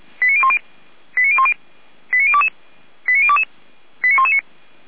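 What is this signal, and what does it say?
Five-tone selective call (ZVEI), heard as quick runs of five tones, tones of about 70 ms. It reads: 89010 89010 89022 89022 89119
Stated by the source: ZVEI five-tone call received on SDR.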